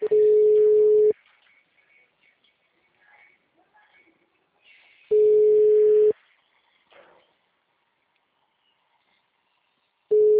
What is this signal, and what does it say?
Ringback tone (Europe, 425 Hz).